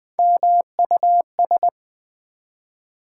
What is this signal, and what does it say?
Morse code, heard as a keyed tone, about 20 wpm, 700 Hz. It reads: MUS
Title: Build A Word - 1 Letter At A Time - 5-Letter Words - 20wpm